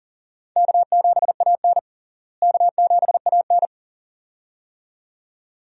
Morse code, heard as a keyed tone, about 40 wpm, 700 Hz. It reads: K7AN K7AN
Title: USA Call Signs - 4 Characters - Repeated 2x - 40wpm